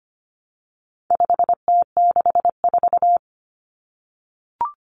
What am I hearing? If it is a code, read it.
Morse code, heard as a keyed tone, about 25 wpm, 700 Hz. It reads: 5T64
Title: Common Cut Numbers - 4-Digit 25wpm